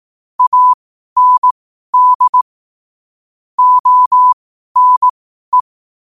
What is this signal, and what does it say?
Morse code, heard as a keyed tone, about 18 wpm, spaced as for 12 wpm, 1000 Hz. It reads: AND ONE
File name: qotdfiles / 12wpm